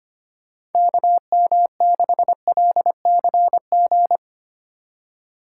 Morse code, heard as a keyed tone, about 32 wpm, 700 Hz. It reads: KM6LCG